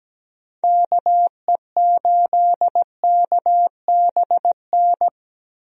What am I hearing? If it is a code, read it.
Morse code, heard as a keyed tone, about 17 wpm, 700 Hz. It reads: KE8KBN
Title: USA Call Signs - 6 Characters - 17wpm